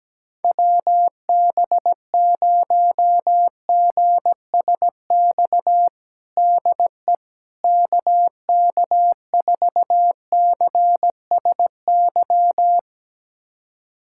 Morse code, heard as a keyed tone, about 17 wpm, 700 Hz. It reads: WB0GSX DE KK4CSY
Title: USA Callsign DE Callsign - 6 Characters - 17wpm